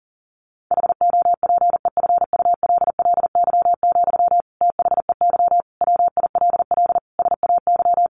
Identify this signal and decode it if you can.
Morse code, heard as a keyed tone, about 37 wpm, 700 Hz. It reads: HOPEFULLY, THEY WILL SAY
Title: Sentences from Top 1500 Words 40wpm